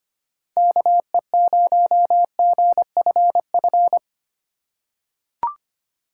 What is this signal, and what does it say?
Morse code, heard as a keyed tone, about 25 wpm, 700 Hz. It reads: KE0GFF
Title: USA Call Signs - 6x Character Spacing - 25wpm